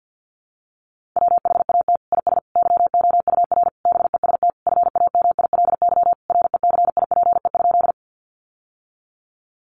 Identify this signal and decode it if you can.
Morse code, heard as a keyed tone, about 50 wpm, 700 Hz. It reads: WHAT IS YOUR BEST FAMILY RECIPE?
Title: Sentences from Top 1200 Words 50wpm